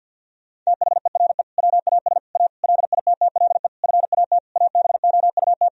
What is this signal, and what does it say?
Morse code, heard as a keyed tone, about 50 wpm, 700 Hz. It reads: THERE WAS A LITTLE FAT ABOUT
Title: Sentences from Top 1000 Words - Review Part 4 - 50wpm